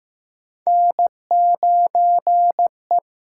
Morse code, heard as a keyed tone, about 15 wpm, 700 Hz. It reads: N9E
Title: USA Call Signs - 4 Characters - 6x Character Spacing - 15wpm